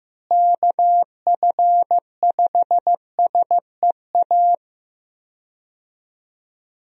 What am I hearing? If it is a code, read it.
Morse code, heard as a keyed tone, about 15 wpm, 700 Hz. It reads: KF5SEA
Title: USA Call Signs - 6 Characters - Repeated 3x - 15wpm